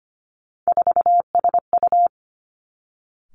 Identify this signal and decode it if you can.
Morse code, heard as a keyed tone, about 25 wpm, 700 Hz. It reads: 4SU
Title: Three Challenging Letter-Number Combinations 25wpm